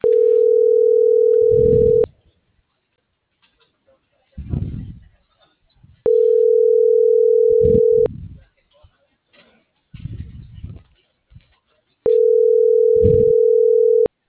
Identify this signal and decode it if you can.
Ringback tone (North America), 440 + 480 Hz.